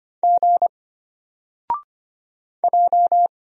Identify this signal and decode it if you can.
Morse code, heard as a keyed tone, about 25 wpm, 700 Hz. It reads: G J
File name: Single-Letter-Number-25wpm